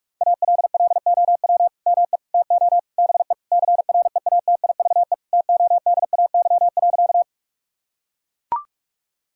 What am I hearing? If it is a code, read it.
Morse code, heard as a keyed tone, about 41 wpm, 700 Hz. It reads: ALLOW ME TO BE CREATIVE TODAY.
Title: Sentences from Top 1000 Words - Review Part 5 - 45wpm